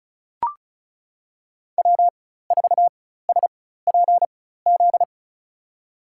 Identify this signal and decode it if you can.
Morse code, heard as a keyed tone, about 34 wpm, spaced as for 16 wpm, 700 Hz. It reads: W4SPZ
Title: USA Call Signs - 4x Character Spacing - 35wpm